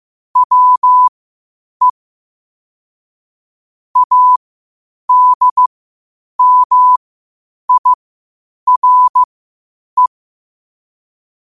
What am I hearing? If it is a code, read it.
Morse code, heard as a keyed tone, about 15 wpm, spaced as for 8 wpm, 1000 Hz. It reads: WE ADMIRE
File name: qotdfiles / 8wpm